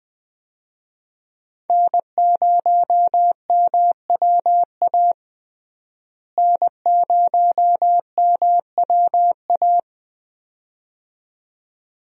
Morse code, heard as a keyed tone, about 21 wpm, 700 Hz. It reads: N0MWA N0MWA
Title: USA Call Signs - 5 Characters - Repeated 2x - 20wpm